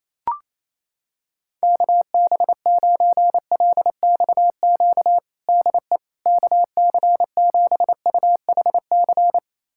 Morse code, heard as a keyed tone, about 27 wpm, 700 Hz. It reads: KB9LXQ DE KC7UHC
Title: USA Callsign DE Callsign - 6 Characters - 28wpm